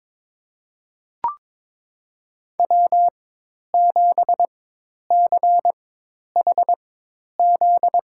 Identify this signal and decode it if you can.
Morse code, heard as a keyed tone, about 22 wpm, spaced as for 10 wpm, 700 Hz. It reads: W7CHZ
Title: USA Call Signs - 4x Character Spacing - 22wpm